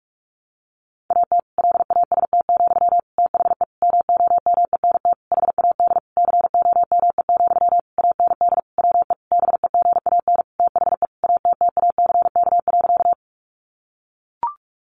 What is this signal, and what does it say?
Morse code, heard as a keyed tone, about 44 wpm, 700 Hz. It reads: AT LAST, THE MOMENT HAD COME, AND WE BEGAN THE ATTACK.
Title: Sentences from Top 1500 Words 45wpm